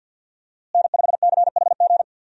Morse code, heard as a keyed tone, about 46 wpm, 700 Hz. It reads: N5CHG
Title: USA Call Signs - 5 Characters - Repeated 2x - 50wpm